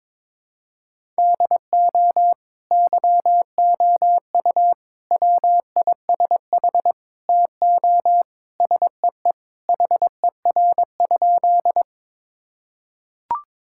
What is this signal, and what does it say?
Morse code, heard as a keyed tone, about 22 wpm, 700 Hz. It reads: DO YOU WISH TO SEE HER?
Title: Sentences from Top 1600 Words 22wpm